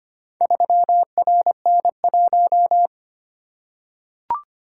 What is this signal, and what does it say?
Morse code, heard as a keyed tone, about 25 wpm, 700 Hz. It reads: 3RN1
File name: practice / 25wpm / Four Letter-Number Combinations - 25wpm